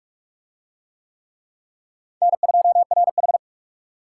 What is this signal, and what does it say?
Morse code, heard as a keyed tone, about 40 wpm, 700 Hz. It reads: N2RH